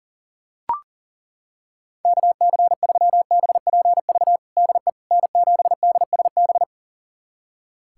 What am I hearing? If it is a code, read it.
Morse code, heard as a keyed tone, about 40 wpm, 700 Hz. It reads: KC3BPV DE N7DSB
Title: USA Callsign DE Callsign 40wpm